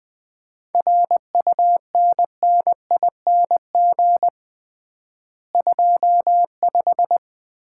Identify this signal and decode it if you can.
Morse code, heard as a keyed tone, about 21 wpm, 700 Hz. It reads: RUNNING 25
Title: Common QSO Phrases - 3x Word-Spacing - 20wpm